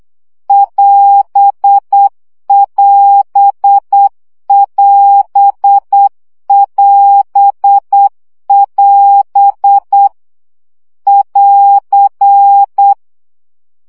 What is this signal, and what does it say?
Morse code, heard as a keyed tone, about 8 wpm, 800 Hz. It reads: <AS><AS><AS><AS><AS> <AR>
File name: CW gyakorlas as 7 baud